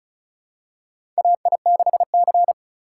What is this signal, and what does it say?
Morse code, heard as a keyed tone, about 36 wpm, 700 Hz. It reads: AI6C
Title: USA Call Signs - 3x Character Spacing - 35wpm